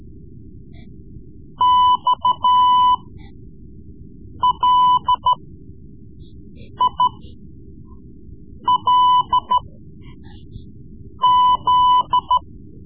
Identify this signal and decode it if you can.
Morse code, heard as a keyed tone, about 13 wpm, spaced as for 5 wpm, 1000 Hz. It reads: XLILZ
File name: Morse Lessons / Morse Lessons / Tape-3B